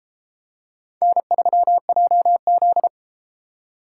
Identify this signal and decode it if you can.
Morse code, heard as a keyed tone, about 33 wpm, 700 Hz. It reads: N3JZ